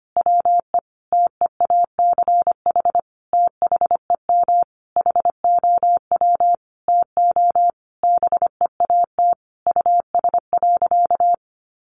Morse code, heard as a keyed tone, about 25 wpm, 700 Hz. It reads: WE TEACH THEM HOW TO BEAT US.
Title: Sentences from Top 1300 Words 25wpm